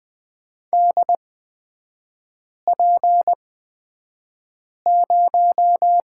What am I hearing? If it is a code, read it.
Morse code, heard as a keyed tone, about 20 wpm, spaced as for 5 wpm, 690 Hz. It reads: DP0
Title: 229 Lesson 37e - Letter x - Three Characters 20wpm